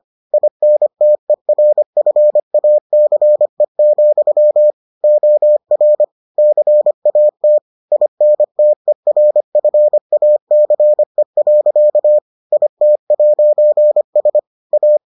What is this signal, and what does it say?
Morse code, heard as a keyed tone, about 25 wpm, 600 Hz. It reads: INTERFACE, OR CAT INTERFACE. IT'S A